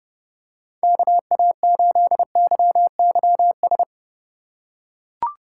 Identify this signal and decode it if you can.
Morse code, heard as a keyed tone, about 30 wpm, 700 Hz. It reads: KA8YYS